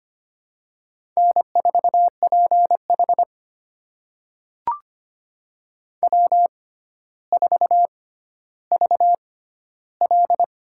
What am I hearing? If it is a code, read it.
Morse code, heard as a keyed tone, about 25 wpm, spaced as for 13 wpm, 700 Hz. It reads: N4PH W4VL
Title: USA Call Signs - 4 Characters - 6x Character Spacing - 25wpm